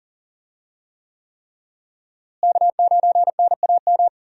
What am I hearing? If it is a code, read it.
Morse code, heard as a keyed tone, about 37 wpm, 700 Hz. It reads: K9NAM
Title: USA Call Signs - 5 Characters - Repeated 3x - 40wpm